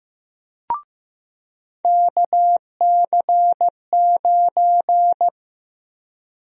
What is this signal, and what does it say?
Morse code, heard as a keyed tone, about 16 wpm, 700 Hz. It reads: KC9
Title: USA Call Sign Prefixes - Repeated 2x - 15wpm